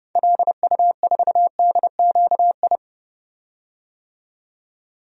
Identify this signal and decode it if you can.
Morse code, heard as a keyed tone, about 30 wpm, 700 Hz. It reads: LU4DQI
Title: DXCC-100 Call Signs - 6 Characters - 30wpm